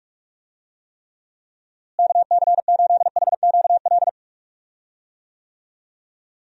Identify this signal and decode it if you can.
Morse code, heard as a keyed tone, about 46 wpm, 700 Hz. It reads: KC8HQL
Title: USA Call Signs - 6 Characters - Repeated 2x - 45wpm